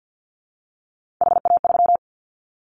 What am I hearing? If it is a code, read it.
Morse code, heard as a keyed tone, about 50 wpm, 700 Hz. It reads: HA3
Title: Callsign Prefixes for Top 50 DXCC - 50wpm